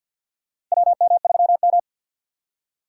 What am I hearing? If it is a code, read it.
Morse code, heard as a keyed tone, about 46 wpm, 700 Hz. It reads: WM3M
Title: USA Call Signs - 4 Characters - 2x Character Spacing - 50wpm